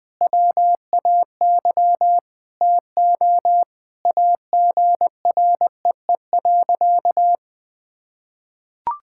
Morse code, heard as a keyed tone, about 20 wpm, 700 Hz. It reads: WAY TO AGREE.
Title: Sentences with 5-Word Phrases - Spoken Beginning - 20wpm